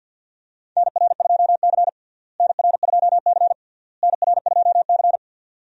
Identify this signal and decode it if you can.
Morse code, heard as a keyed tone, about 48 wpm, 700 Hz. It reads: NR2C NR2C NR2C
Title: USA Call Signs - 4 Characters - Repeated 3x - 50wpm